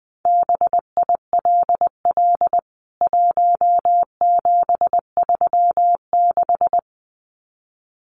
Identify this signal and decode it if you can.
Morse code, heard as a keyed tone, about 20 wpm, 700 Hz. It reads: BILL 1736
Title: CWT Exchange 20wpm